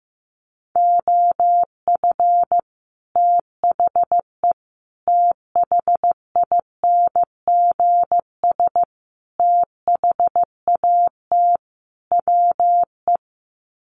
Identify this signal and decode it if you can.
Morse code, heard as a keyed tone, about 15 wpm, 700 Hz. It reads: OF THE THINGS THAT WE
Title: Sets of 5 Words Encore 15wpm